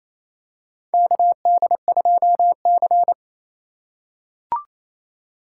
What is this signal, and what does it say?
Morse code, heard as a keyed tone, about 28 wpm, 700 Hz. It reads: KD2C
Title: USA Call Signs - 4x Character Spacing - 28wpm